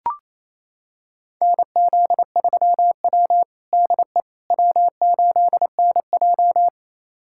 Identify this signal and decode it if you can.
Morse code, heard as a keyed tone, about 28 wpm, 700 Hz. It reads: NZ3W DE W8NJ
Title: USA Callsign DE Callsign - 4 Characters - 28wpm